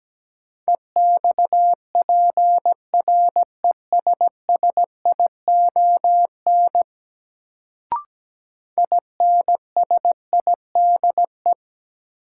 Morse code, heard as a keyed tone, about 17 wpm, 700 Hz. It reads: EXPRESSION INSIDE
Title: Sentences from Top 1200 Words - New Word Drill Alt - 17wpm